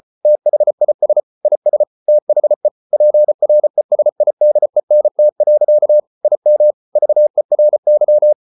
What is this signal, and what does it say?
Morse code, heard as a keyed tone, about 34 wpm, 600 Hz. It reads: THIS IS THE PRESIDENT. IM VERY